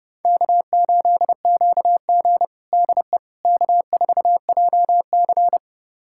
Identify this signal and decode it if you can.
Morse code, heard as a keyed tone, about 32 wpm, 700 Hz. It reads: K8QG DE K4JC
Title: USA Callsign DE Callsign - 4 Characters - 30wpm